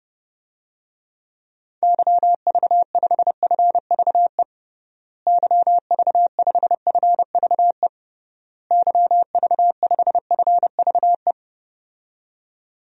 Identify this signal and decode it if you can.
Morse code, heard as a keyed tone, about 31 wpm, 700 Hz. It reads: YV5FVE YV5FVE YV5FVE
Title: DXCC-100 Call Signs - 6 Characters - Repeated 3x - 30wpm